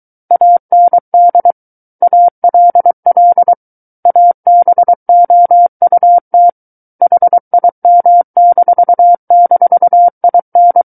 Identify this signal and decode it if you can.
Morse code, heard as a keyed tone, about 23 wpm, 700 Hz. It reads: AND ALL ABOUT HIM--IN
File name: Book One / 14-Chapter 14 - 23 WPM